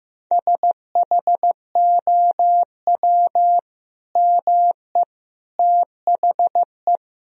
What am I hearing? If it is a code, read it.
Morse code, heard as a keyed tone, about 15 wpm, 700 Hz. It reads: SHOW ME THE
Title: Sentences from Top 1000 Words - Review Part 1 - 15wpm